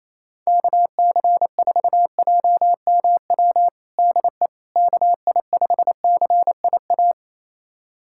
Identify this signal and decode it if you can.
Morse code, heard as a keyed tone, about 28 wpm, 700 Hz. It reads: KC4JMW DE KI5CIA
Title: USA Callsign DE Callsign 28wpm